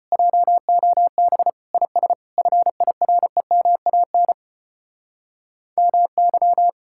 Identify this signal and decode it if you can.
Morse code, heard as a keyed tone, about 33 wpm, 700 Hz. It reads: JOB IS FIREMAN MY